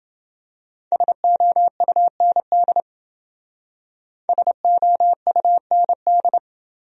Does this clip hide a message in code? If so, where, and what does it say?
Yes: Morse code from 0.9 to 6.4 s, heard as a keyed tone, about 28 wpm, 700 Hz. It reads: SOUND SOUND